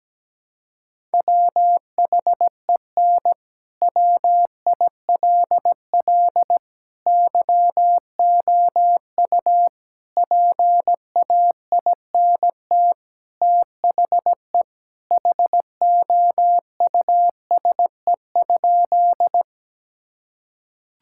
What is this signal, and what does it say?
Morse code, heard as a keyed tone, about 17 wpm, 700 Hz. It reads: WHEN WILL YOU PAINT THE HOUSE?
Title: Sentences from Top 1000 Words - Review Part 6 - 17wpm